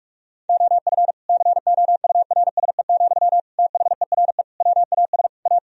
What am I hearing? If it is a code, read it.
Morse code, heard as a keyed tone, about 46 wpm, 700 Hz. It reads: OF COURSE, THERE WAS A
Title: Sentences from Top 1000 Words - Review Part 4 - 45wpm